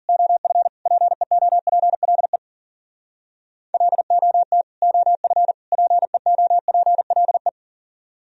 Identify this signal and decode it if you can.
Morse code, heard as a keyed tone, about 46 wpm, 700 Hz. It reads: OF PEOPLE LOT OF PEOPLE
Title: Speed-Racing - Sets of 3 Words 40wpm